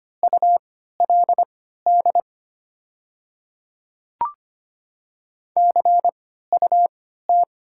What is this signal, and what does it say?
Morse code, heard as a keyed tone, about 25 wpm, spaced as for 14 wpm, 690 Hz. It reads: ULD CUT
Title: 3x Character Spacing - Top 500 Words - 25wpm